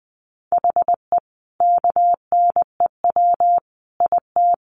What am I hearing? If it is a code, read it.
Morse code, heard as a keyed tone, about 20 wpm, 700 Hz. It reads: HE KNEW IT
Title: Sentences from Top 1200 Words 20wpm